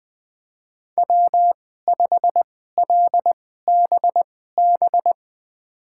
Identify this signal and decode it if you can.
Morse code, heard as a keyed tone, about 20 wpm, spaced as for 15 wpm, 700 Hz. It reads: W5LBB
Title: USA Call Signs - 5 Characters - 2x Character Spacing - 20wpm